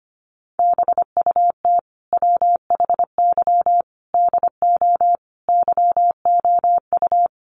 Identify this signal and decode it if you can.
Morse code, heard as a keyed tone, about 25 wpm, 700 Hz. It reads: BUT WHY DO YOU
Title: Sentences from Top 200 Words 25wpm